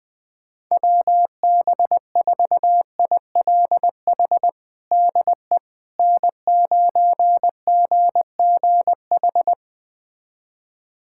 Morse code, heard as a keyed tone, about 20 wpm, 700 Hz. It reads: WB4ILH DE N9GGH
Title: USA Callsign DE Callsign 20wpm